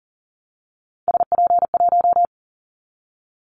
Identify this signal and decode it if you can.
Morse code, heard as a keyed tone, about 40 wpm, 700 Hz. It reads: SP1
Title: Callsign Prefixes for Top 25 DXCC - Repeated 3x - 40wpm